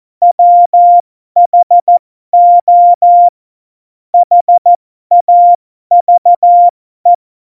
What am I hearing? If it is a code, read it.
Morse code, heard as a keyed tone, about 14 wpm, spaced as for 12 wpm, 700 Hz. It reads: WHO HAVE
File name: Morsekurs / Witze